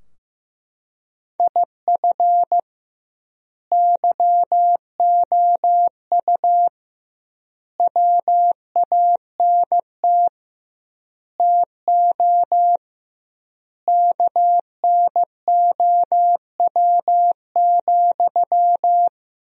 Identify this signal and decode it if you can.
Morse code, heard as a keyed tone, about 16 wpm, 700 Hz. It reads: IF YOU WANT TO KNOW,
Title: 2x Word-Spacing - Sentences from Top 200 Words - 15wpm